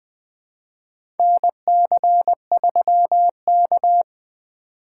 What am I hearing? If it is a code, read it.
Morse code, heard as a keyed tone, about 20 wpm, 700 Hz. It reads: NC3K
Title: USA Call Signs - 4 Characters - 2x Character Spacing - 20wpm